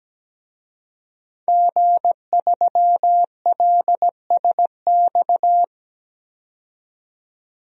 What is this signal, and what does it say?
Morse code, heard as a keyed tone, about 17 wpm, 700 Hz. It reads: G3LSX
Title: DXCC-100 Call Signs - 5 Characters - 17wpm